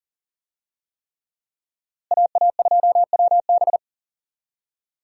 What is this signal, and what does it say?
Morse code, heard as a keyed tone, about 40 wpm, 700 Hz. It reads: AA2WB
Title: USA Call Signs - 5 Characters - Repeated 3x - 40wpm